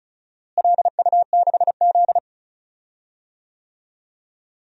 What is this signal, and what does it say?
Morse code, heard as a keyed tone, about 36 wpm, 700 Hz. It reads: LU6Z
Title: DXCC-100 Call Signs - 4 Characters - Repeated 2x - 35wpm